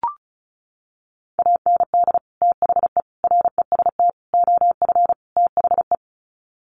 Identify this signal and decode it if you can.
Morse code, heard as a keyed tone, about 34 wpm, 700 Hz. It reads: AND THE REST OF THE
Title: Sets of 5 Words 35wpm